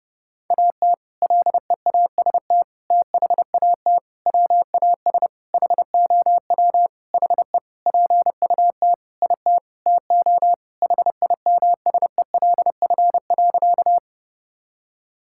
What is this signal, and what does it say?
Morse code, heard as a keyed tone, about 32 wpm, 700 Hz. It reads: AT LEAST THAT WAS HOW HE PUT IT TO HIMSELF.
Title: Sentences from Top 1600 Words 30wpm